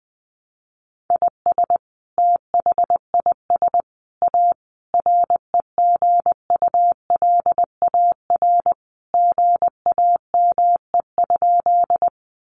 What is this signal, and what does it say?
Morse code, heard as a keyed tone, about 20 wpm, 700 Hz. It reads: IS THIS A REGULAR GAME?
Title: Sentences from Top 900 Words 20wpm